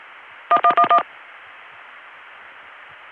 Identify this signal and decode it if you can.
DTMF, heard as telephone touch tones, tones of about 80 ms. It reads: 1111